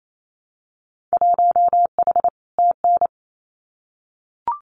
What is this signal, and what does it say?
Morse code, heard as a keyed tone, about 28 wpm, 700 Hz. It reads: 1H TN
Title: Winter Field Day Exchanges 28wpm